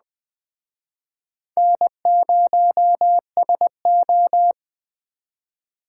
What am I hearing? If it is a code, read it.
Morse code, heard as a keyed tone, about 20 wpm, 700 Hz. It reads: N0SO
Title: Speed Racing - USA 1x2 Call Signs 17wpm